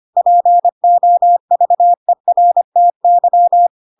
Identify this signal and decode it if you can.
Morse code, heard as a keyed tone, about 25 wpm, 700 Hz. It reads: POVERTY